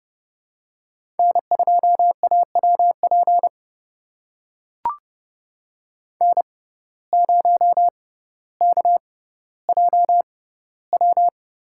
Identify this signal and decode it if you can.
Morse code, heard as a keyed tone, about 30 wpm, spaced as for 15 wpm, 700 Hz. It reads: N2AWP N0KJW
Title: USA Call Signs - 5 Characters - 6x Character Spacing - 30wpm